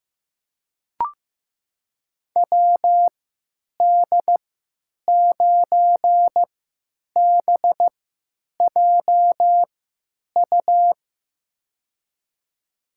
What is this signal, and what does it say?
Morse code, heard as a keyed tone, about 15 wpm, spaced as for 9 wpm, 700 Hz. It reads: WD9BJU
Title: USA Call Signs - 6 Characters - 3x Character Spacing - 15wpm